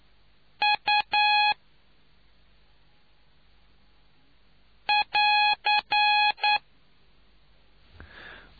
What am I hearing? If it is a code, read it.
Morse code, heard as a keyed tone, about 9 wpm, 840 Hz. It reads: U <AR>